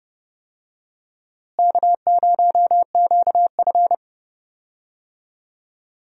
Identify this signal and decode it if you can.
Morse code, heard as a keyed tone, about 31 wpm, 700 Hz. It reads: K0QF
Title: USA Call Signs - 4 Characters - Repeated 2x - 30wpm